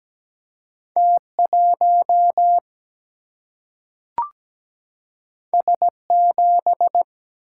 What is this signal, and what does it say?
Morse code, heard as a keyed tone, about 17 wpm, 700 Hz. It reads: T1 S7
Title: Single-Letter Single-Number 17wpm